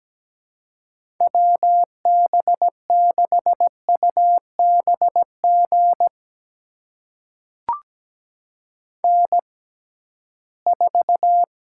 Morse code, heard as a keyed tone, about 17 wpm, spaced as for 13 wpm, 700 Hz. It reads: WB6UBG N4
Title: USA Call Signs - 6x Character Spacing - 17wpm